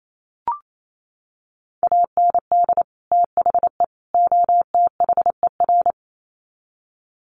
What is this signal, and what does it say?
Morse code, heard as a keyed tone, about 28 wpm, 700 Hz. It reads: AND THE OTHER
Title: Sets of 3 Words 28wpm